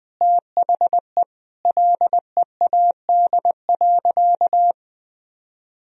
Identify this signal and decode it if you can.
Morse code, heard as a keyed tone, about 20 wpm, 700 Hz. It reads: THE LEAD.
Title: Sentences from Top 1000 Words - Review Part 2 - 20wpm